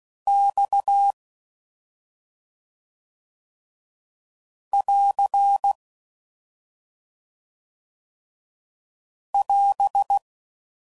Morse code, heard as a keyed tone, about 16 wpm, 800 Hz. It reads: X <AR> <AS>